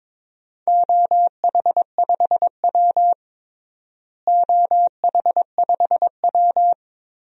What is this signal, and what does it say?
Morse code, heard as a keyed tone, about 22 wpm, 700 Hz. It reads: OH5W OH5W
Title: DXCC-100 Call Signs - 4 Characters - Repeated 3x - 22wpm